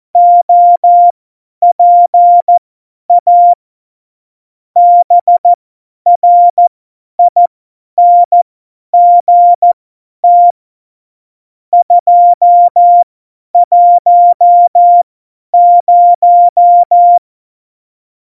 Morse code, heard as a keyed tone, about 14 wpm, spaced as for 10 wpm, 700 Hz. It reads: OPA BRINGT 210
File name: Morsekurs / Witze / Witz 27. Oktober 2020, 50 F70